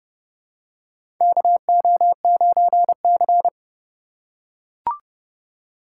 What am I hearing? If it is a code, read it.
Morse code, heard as a keyed tone, about 30 wpm, 700 Hz. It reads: KO9C
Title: USA Call Signs - 4x Character Spacing - 30wpm